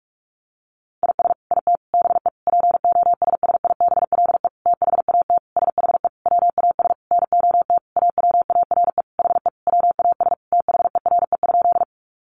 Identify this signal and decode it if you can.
Morse code, heard as a keyed tone, about 44 wpm, 700 Hz. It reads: IS IT BE POSSIBLE THAT SHE WAS NOT AWARE HE WAS THERE?
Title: Sentences from Top 1200 Words 45wpm